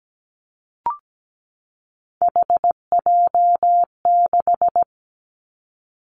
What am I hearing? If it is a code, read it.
Morse code, heard as a keyed tone, about 17 wpm, 700 Hz. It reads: HJ6
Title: Callsign Prefixes for Top 100 DXCC - 17wpm